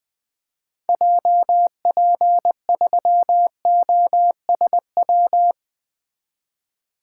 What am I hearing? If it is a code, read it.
Morse code, heard as a keyed tone, about 20 wpm, 700 Hz. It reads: JP3OSW
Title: 2023 SOTA Activator Call Signs - 20wpm